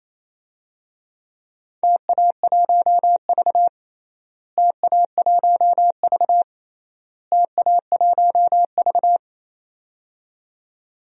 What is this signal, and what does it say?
Morse code, heard as a keyed tone, about 28 wpm, 700 Hz. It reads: TA1V TA1V TA1V
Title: DXCC-100 Call Signs - 4 Characters - Repeated 3x - 28wpm